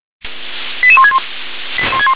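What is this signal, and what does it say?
Five-tone selective call (ZVEI), heard as quick runs of five tones, tones of about 80 ms. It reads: 88161
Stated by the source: SDR-received ZVEI five-tone selective call, code 88154.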